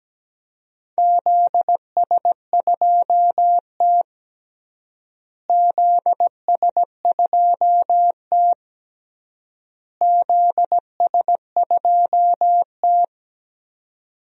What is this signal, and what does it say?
Morse code, heard as a keyed tone, about 17 wpm, 700 Hz. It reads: ZS2T ZS2T ZS2T
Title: DXCC-100 Call Signs - 4 Characters - Repeated 3x - 17wpm